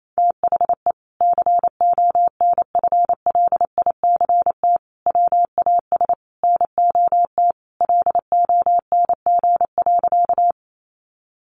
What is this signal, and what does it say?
Morse code, heard as a keyed tone, about 28 wpm, 700 Hz. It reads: THE CONFLICT WAS NOT LONG.
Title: Sentences from Top 1300 Words 28wpm